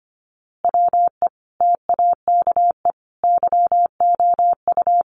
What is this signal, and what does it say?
Morse code, heard as a keyed tone, about 25 wpm, 700 Hz. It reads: WE TAKE YOU
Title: A Sound Of Thunder 25wpm